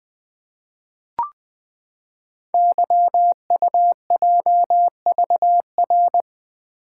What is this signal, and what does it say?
Morse code, heard as a keyed tone, about 20 wpm, 700 Hz. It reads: YUJVR
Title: Code Groups - 5 Letters - 20wpm